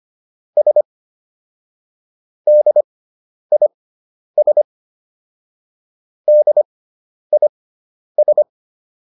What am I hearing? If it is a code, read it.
Morse code, heard as a keyed tone, about 25 wpm, spaced as for 10 wpm, 600 Hz. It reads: S DIS DIS